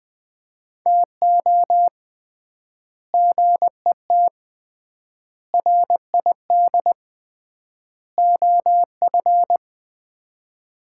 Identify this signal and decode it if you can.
Morse code, heard as a keyed tone, about 21 wpm, 700 Hz. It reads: TO GET RID OF
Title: 3x Word-Spacing - Sets of 4 Words - 20wpm